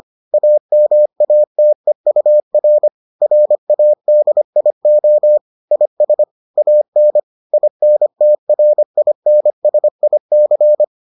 Morse code, heard as a keyed tone, about 25 wpm, 600 Hz. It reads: AMATEUR RADIO IS AN INTRINSIC